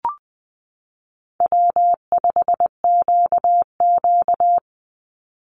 Morse code, heard as a keyed tone, about 20 wpm, 700 Hz. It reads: W5QQ
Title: Call Signs Repeated 2x - 20wpm